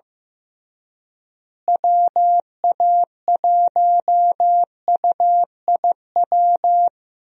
Morse code, heard as a keyed tone, about 15 wpm, 700 Hz. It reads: WA1UIW
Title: USA Call Signs - 6 Characters - Repeated 3x - 15wpm